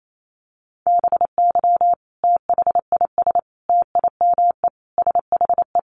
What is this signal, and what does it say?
Morse code, heard as a keyed tone, about 28 wpm, 700 Hz. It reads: BY THIS TIME SHE
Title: Sentences from Top 700 Words 28wpm